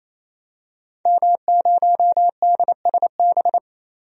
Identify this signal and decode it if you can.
Morse code, heard as a keyed tone, about 27 wpm, 700 Hz. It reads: M0DSB
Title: DXCC-100 Call Signs - 5 Characters - 28wpm